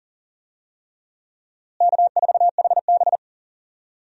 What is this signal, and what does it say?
Morse code, heard as a keyed tone, about 38 wpm, 700 Hz. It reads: K4HB
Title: USA Call Signs - 4 Characters - Repeated 3x - 40wpm